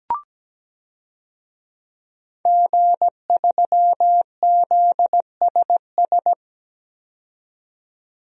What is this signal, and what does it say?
Morse code, heard as a keyed tone, about 17 wpm, 700 Hz. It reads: G3ZSS
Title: DXCC-100 Call Signs - 5 Characters - 17wpm